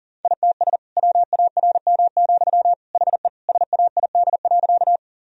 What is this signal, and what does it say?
Morse code, heard as a keyed tone, about 40 wpm, 700 Hz. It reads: ITS WARM, HE SAID.
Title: Sentences from Top 1000 Words - Review Part 2 - 40wpm